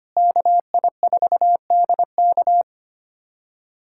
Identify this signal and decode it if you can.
Morse code, heard as a keyed tone, about 25 wpm, 700 Hz. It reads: KI4DK
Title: K1USN SST QSO Practice - 25wpm